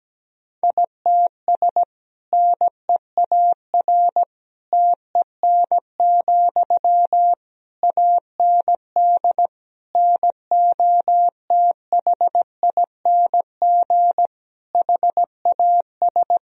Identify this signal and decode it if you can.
Morse code, heard as a keyed tone, about 17 wpm, 700 Hz. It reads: ITS NEAR TEN, AND NOTHING HAS
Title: Sentences from Top 1000 Words - Review Part 2 - 17wpm